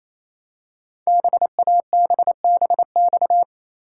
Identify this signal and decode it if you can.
Morse code, heard as a keyed tone, about 28 wpm, 700 Hz. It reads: BABBX
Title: Code Groups - 5 Letters - 28wpm